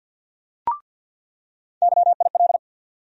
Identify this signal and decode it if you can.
Morse code, heard as a keyed tone, about 48 wpm, 700 Hz. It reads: YIL
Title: USA Call Sign Postfixes - 3 Letters 50wpm